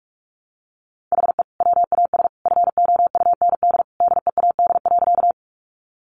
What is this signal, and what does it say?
Morse code, heard as a keyed tone, about 44 wpm, 700 Hz. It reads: HE WAS FOUND DEAD.
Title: Sentences from Top 1500 Words 45wpm